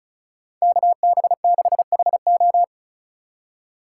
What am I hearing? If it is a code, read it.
Morse code, heard as a keyed tone, about 43 wpm, 700 Hz. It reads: KB6HO
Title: Speed Racing - USA Call Signs 35wpm